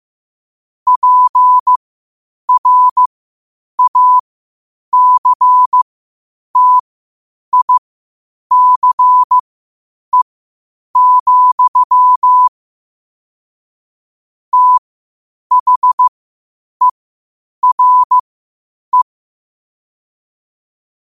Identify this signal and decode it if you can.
Morse code, heard as a keyed tone, about 15 wpm, spaced as for 8 wpm, 1000 Hz. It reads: PRACTICE, THERE
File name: qotdfiles / 8wpm